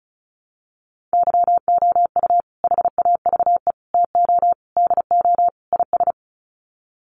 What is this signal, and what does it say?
Morse code, heard as a keyed tone, about 34 wpm, 700 Hz. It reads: YOU HAVE TO DO IS
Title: Sets of 5 Words 35wpm